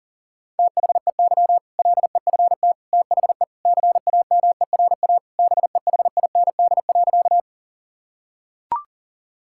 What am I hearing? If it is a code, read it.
Morse code, heard as a keyed tone, about 40 wpm, 700 Hz. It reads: THEY LEFT THE CAMERA BEHIND.
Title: Sentences from Top 1000 Words - Review Part 5 - 40wpm